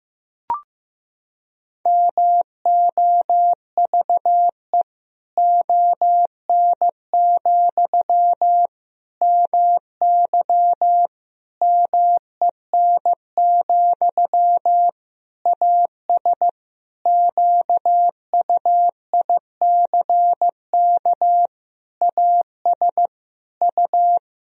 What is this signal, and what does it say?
Morse code, heard as a keyed tone, about 15 wpm, 700 Hz. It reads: MOVE ON, MY MEN, AS QUICK AS U
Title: Sentences from Top 500 Words and Abbreviations 15wpm